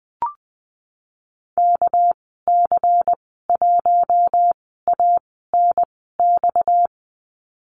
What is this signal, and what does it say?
Morse code, heard as a keyed tone, about 20 wpm, spaced as for 15 wpm, 700 Hz. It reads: KC1ANX
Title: 238 Lesson 38g - Letter q - Call Signs 20wpm